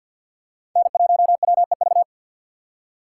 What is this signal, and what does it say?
Morse code, heard as a keyed tone, about 53 wpm, 700 Hz. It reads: N1WEV
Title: USA Call Signs - 5 Characters - 4x Character Spacing - 50wpm